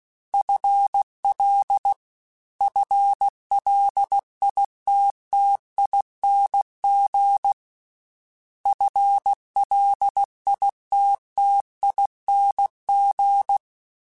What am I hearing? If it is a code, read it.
Morse code, heard as a keyed tone, about 16 wpm, 800 Hz. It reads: FL FLITTING FLITTING
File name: FL / FL mp3s / FL Prefixes 3x2 @15